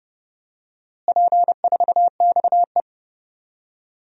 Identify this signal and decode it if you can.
Morse code, heard as a keyed tone, about 30 wpm, 700 Hz. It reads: P4XE
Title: DXCC-100 Call Signs - 4 Characters - Repeated 3x - 30wpm